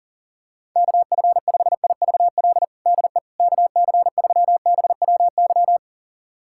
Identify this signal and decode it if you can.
Morse code, heard as a keyed tone, about 40 wpm, 700 Hz. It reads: KF5IVL DE KC3BWY
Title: USA Callsign DE Callsign - 6 Characters - 40wpm